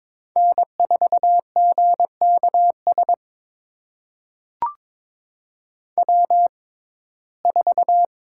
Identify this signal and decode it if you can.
Morse code, heard as a keyed tone, about 22 wpm, spaced as for 16 wpm, 700 Hz. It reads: N4GKS W4
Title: USA Call Signs - 5 Characters - 6x Character Spacing - 22wpm